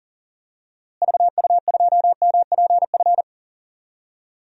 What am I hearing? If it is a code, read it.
Morse code, heard as a keyed tone, about 40 wpm, 700 Hz. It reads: VU2MPF
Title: DXCC-100 Call Signs - 6 Characters - Repeated 2x - 40wpm